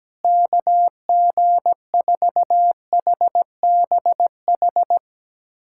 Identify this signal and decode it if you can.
Morse code, heard as a keyed tone, about 17 wpm, 700 Hz. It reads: KG4HBH
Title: USA Callsign DE Callsign 17wpm